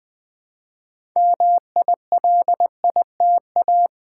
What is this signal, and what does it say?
Morse code, heard as a keyed tone, about 20 wpm, 700 Hz. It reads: MILITA
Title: Build A Word - 2 Letters At A Time - 8-Letter Words - 20wpm